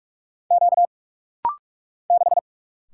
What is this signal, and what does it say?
Morse code, heard as a keyed tone, about 48 wpm, 700 Hz. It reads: Q 6
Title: Single Letter-Number - Rapid-Fire 45wpm